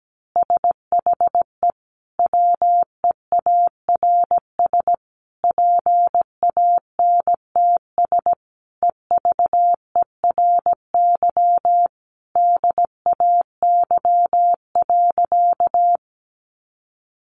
Morse code, heard as a keyed tone, about 17 wpm, 700 Hz. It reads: SHE WEARS PANTS EVERY DAY.
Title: Additional Easy Phrases 17wpm